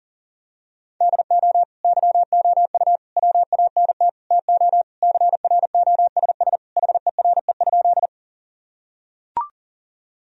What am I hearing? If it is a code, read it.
Morse code, heard as a keyed tone, about 40 wpm, 700 Hz. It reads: DO YOU WANT TO CROSS HERE?
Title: Sentences from Top 1000 Words - Review Part 1 - 40wpm